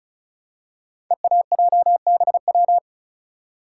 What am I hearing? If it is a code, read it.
Morse code, heard as a keyed tone, about 36 wpm, 690 Hz. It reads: EAJBW
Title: Code Groups - 5 Letters - 35wpm